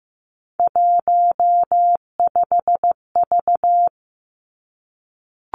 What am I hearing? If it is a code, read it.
Morse code, heard as a keyed tone, about 15 wpm, 700 Hz. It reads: 15V
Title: Three Challenging Letter-Number Combinations 15wpm